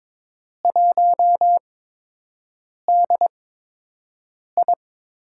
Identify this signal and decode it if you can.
Morse code, heard as a keyed tone, about 22 wpm, spaced as for 6 wpm, 700 Hz. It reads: 1DI
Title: USA Call Signs - 4 Characters - 8x Character Spacing - 22wpm